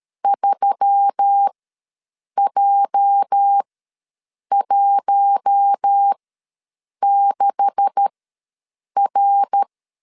Morse code, heard as a keyed tone, about 13 wpm, spaced as for 7 wpm, 800 Hz. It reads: 3J16R
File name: LEZIONI / 9 2867 ws 12wpm 3431